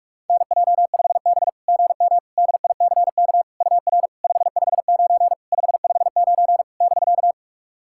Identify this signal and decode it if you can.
Morse code, heard as a keyed tone, about 44 wpm, 690 Hz. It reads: NJ5B GM DICK UR 559 559 <BK>
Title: SOTA QSO Practice 45wpm